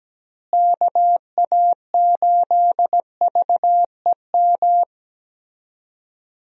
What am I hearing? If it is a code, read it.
Morse code, heard as a keyed tone, about 17 wpm, 700 Hz. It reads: KA8VEM
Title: USA Call Signs - 6 Characters - Repeated 2x - 17wpm